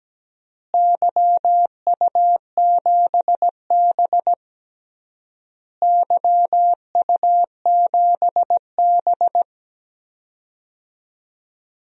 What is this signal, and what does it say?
Morse code, heard as a keyed tone, about 17 wpm, 700 Hz. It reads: YU7B YU7B
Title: DXCC-100 Call Signs - 4 Characters - Repeated 3x - 17wpm